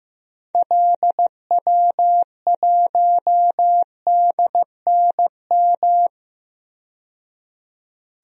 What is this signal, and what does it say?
Morse code, heard as a keyed tone, about 15 wpm, 700 Hz. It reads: LW1DNM
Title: DXCC-100 Call Signs - 6 Characters - 15wpm